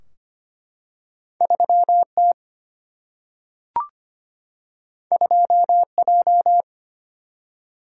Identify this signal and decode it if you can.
Morse code, heard as a keyed tone, about 25 wpm, 700 Hz. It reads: 3T 2J
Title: Single-Letter Single-Number 25wpm